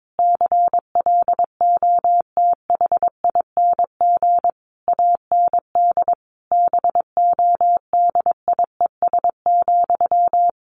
Morse code, heard as a keyed tone, about 22 wpm, 700 Hz. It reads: CLOTHING AND BODIES,
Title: A Sound Of Thunder 22wpm